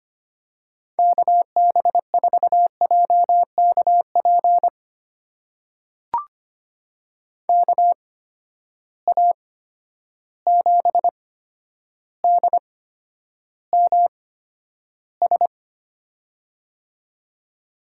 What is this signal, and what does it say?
Morse code, heard as a keyed tone, about 25 wpm, spaced as for 11 wpm, 700 Hz. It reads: KB4JKP KA7DMS
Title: USA Call Signs - 6 Characters - 8x Character Spacing - 25wpm